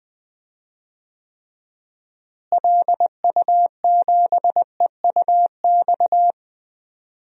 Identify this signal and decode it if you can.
Morse code, heard as a keyed tone, about 20 wpm, 700 Hz. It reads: LU7EUX
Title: DXCC-100 Call Signs - 6 Characters - Repeated 3x - 20wpm